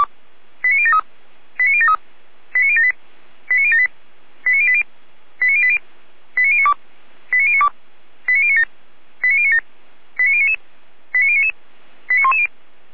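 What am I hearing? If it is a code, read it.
Five-tone selective call (ZVEI), heard as quick runs of five tones, tones of about 70 ms. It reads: 89073 89073 89078 89078 89080 89080 89092 89092 89097 89097 89099 89099 89110
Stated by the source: ZVEI five-tone selective call received on an SDR, from feuerwehr-orlamuende.org (Orlamünde fire brigade).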